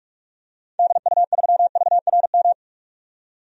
Morse code, heard as a keyed tone, about 46 wpm, 700 Hz. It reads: DU3VRM
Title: DXCC-100 Call Signs - 6 Characters - 45wpm